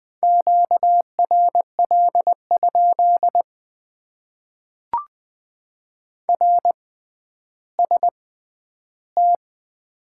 Morse code, heard as a keyed tone, about 20 wpm, spaced as for 11 wpm, 700 Hz. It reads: QRL? RST
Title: Top 90 QSO Elements - 6x Character Spacing - 20wpm